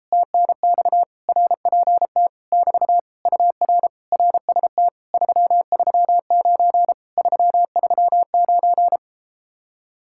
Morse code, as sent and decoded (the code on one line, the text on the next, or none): - -. -..- / .-. .--. - / -...- / ..- .-. / .-. ... - / ...-- ...-- ----. / ...-- ...-- ----.
TNX RPT <BT> UR RST 339 339